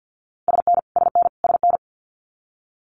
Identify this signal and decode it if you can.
Morse code, heard as a keyed tone, about 48 wpm, 700 Hz. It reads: SN SN SN
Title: Abbreviations Repeated 3x - 50wpm